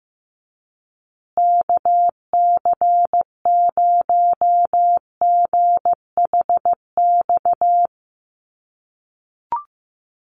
Morse code, heard as a keyed tone, about 15 wpm, 700 Hz. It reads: KC0GHX
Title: USA Call Signs Repeated 3x 15wpm